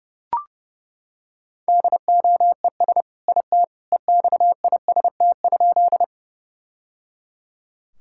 Morse code, heard as a keyed tone, about 30 wpm, 700 Hz. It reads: DOES IT EXIST?
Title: Sentences from Top 1000 Words - Review Part 5 - 30wpm